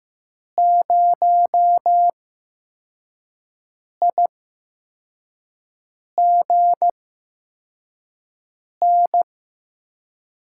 Morse code, heard as a keyed tone, about 15 wpm, spaced as for 4 wpm, 700 Hz. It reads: 0IGN
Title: USA Call Signs - 8x Character Spacing - 15wpm